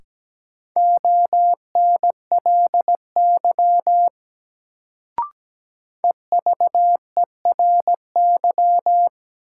Morse code, heard as a keyed tone, about 17 wpm, 700 Hz. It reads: ONLY EVERY
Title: Sentences from Top 200 Words - New Word Drill - 17wpm